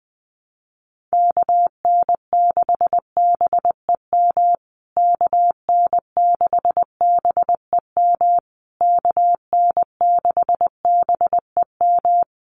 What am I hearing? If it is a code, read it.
Morse code, heard as a keyed tone, about 20 wpm, 700 Hz. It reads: KN6BEM KN6BEM KN6BEM
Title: USA Call Signs Repeated 3x 20wpm